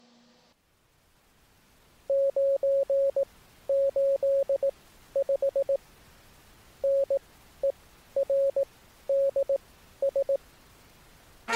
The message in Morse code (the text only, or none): 985NERDS